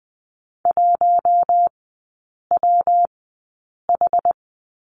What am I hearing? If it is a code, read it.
Morse code, heard as a keyed tone, about 20 wpm, spaced as for 8 wpm, 700 Hz. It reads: 1WH
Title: 114 Lesson 19e - Letter w - Three Characters 20wpm